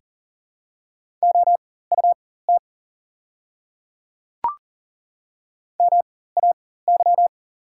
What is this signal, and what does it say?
Morse code, heard as a keyed tone, about 40 wpm, spaced as for 19 wpm, 690 Hz. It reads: OUT MAY
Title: 4x Character Spacing - Top 100 Words - 40wpm